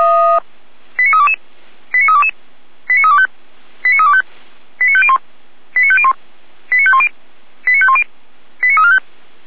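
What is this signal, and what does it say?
Five-tone selective call (ZVEI), heard as quick runs of five tones, tones of about 70 ms. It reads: 89322 89322 89325 89325 89501 89501 89510 89510 89345
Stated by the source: ZVEI five-tone selective call 89305 received on SDR, logged by Feuerwehr Orlamünde.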